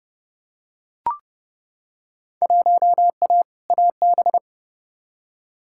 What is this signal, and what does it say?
Morse code, heard as a keyed tone, about 30 wpm, 700 Hz. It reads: 1A AB
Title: Field Day Exchanges - 30wpm